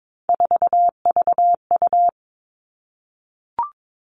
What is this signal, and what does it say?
Morse code, heard as a keyed tone, about 22 wpm, 700 Hz. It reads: 4VU
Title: All Cut Numbers - 3-Digit 22wpm